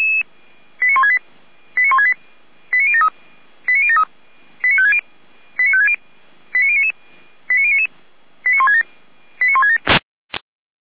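Five-tone selective call (ZVEI), heard as quick runs of five tones, tones of about 70 ms. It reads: 89168 89168 89073 89073 89560 89560 89099 89099 89167 89167
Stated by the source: ZVEI five-tone selective call received on an SDR.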